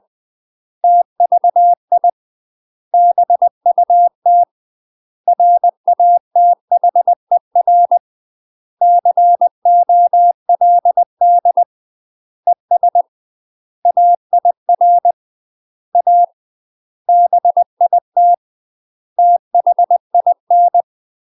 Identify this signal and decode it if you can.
Morse code, heard as a keyed tone, about 20 wpm, 700 Hz. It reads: TVI BUT RATHER COLD ES AIR A BIT THIN